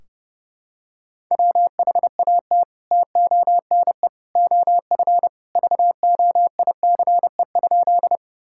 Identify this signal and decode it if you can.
Morse code, heard as a keyed tone, about 30 wpm, 700 Hz. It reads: WHAT TONE OF VOICE?
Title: Sentences from Top 1600 Words 30wpm